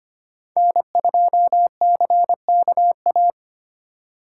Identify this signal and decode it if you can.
Morse code, heard as a keyed tone, about 32 wpm, 700 Hz. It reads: N2CKA